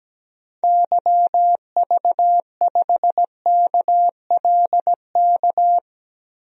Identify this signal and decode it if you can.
Morse code, heard as a keyed tone, about 17 wpm, 700 Hz. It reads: YV5KLK
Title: DXCC-100 Call Signs - 6 Characters - 17wpm